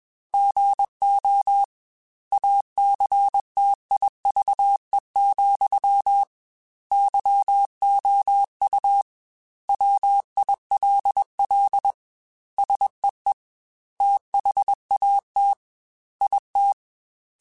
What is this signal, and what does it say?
Morse code, heard as a keyed tone, about 21 wpm, 800 Hz. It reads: GO ACTIVE, YOU WILL SEE THAT IT
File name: Technical Subjects / Technical Subjects-mp3s / OR Gate @20